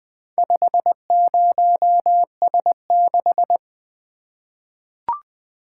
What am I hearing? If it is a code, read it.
Morse code, heard as a keyed tone, about 20 wpm, 700 Hz. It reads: 50S6
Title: Four Letter-Number Combinations - 20wpm